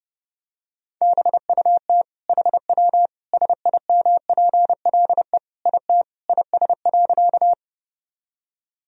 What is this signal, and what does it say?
Morse code, heard as a keyed tone, about 30 wpm, 700 Hz. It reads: BUT HW SIMPLE IT IS.
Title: Sentences from Top 400 Words and Abbreviations 30wpm